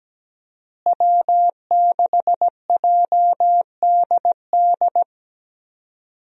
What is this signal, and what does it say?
Morse code, heard as a keyed tone, about 17 wpm, 700 Hz. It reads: W6JDD
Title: USA Call Signs - 5 Characters - Repeated 2x - 17wpm